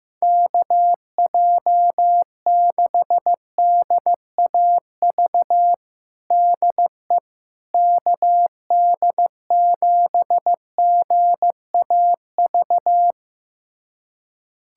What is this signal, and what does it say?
Morse code, heard as a keyed tone, about 15 wpm, 700 Hz. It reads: KJ6DAV DE KD7GAV
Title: USA Callsign DE Callsign - 6 Characters - 15wpm